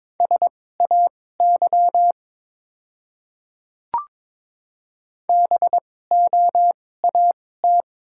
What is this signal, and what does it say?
Morse code, heard as a keyed tone, about 22 wpm, spaced as for 16 wpm, 700 Hz. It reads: SAY BOAT
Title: 2x Character Spacing - Top 500 Words - 22wpm